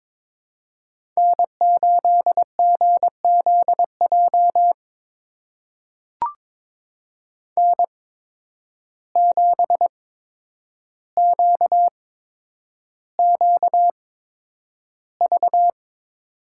Morse code, heard as a keyed tone, about 22 wpm, spaced as for 9 wpm, 700 Hz. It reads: N8GZJ N7QQV
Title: USA Call Signs - 5 Characters - 8x Character Spacing - 22wpm